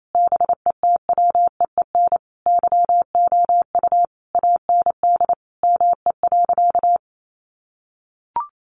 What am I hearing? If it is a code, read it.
Morse code, heard as a keyed tone, about 28 wpm, 700 Hz. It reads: BETWEEN YOU AND ME.
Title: Sentences from Top 1200 Words 28wpm